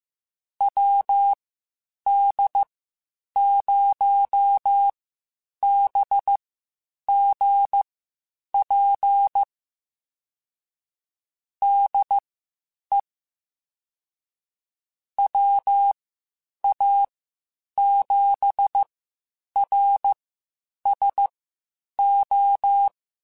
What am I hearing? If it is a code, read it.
Morse code, heard as a keyed tone, about 15 wpm, spaced as for 8 wpm, 800 Hz. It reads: WD0BGP DE WA7RSO
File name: (spaced) / QSOs Spaced-mp3s / WD0BGP 1 @13 sp